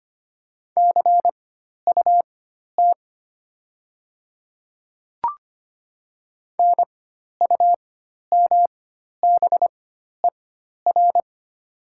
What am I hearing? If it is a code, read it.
Morse code, heard as a keyed tone, about 25 wpm, spaced as for 12 wpm, 700 Hz. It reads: CUT NUMBER
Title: 4x Character Spacing - Top 300 Words - 25wpm